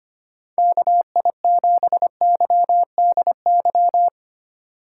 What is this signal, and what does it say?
Morse code, heard as a keyed tone, about 25 wpm, 700 Hz. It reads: KI7YDY
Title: USA Call Signs - 6 Characters - 4x Character Spacing - 25wpm